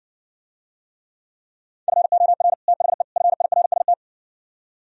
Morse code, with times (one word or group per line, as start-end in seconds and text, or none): 1.9-2.5 FOR
2.7-3.0 THE
3.2-3.9 FIRST